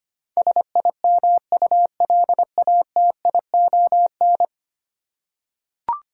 Morse code, heard as a keyed tone, about 25 wpm, 700 Hz. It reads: SIMULATION
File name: 10 Letter Words - First 3 Letters Given - 25wpm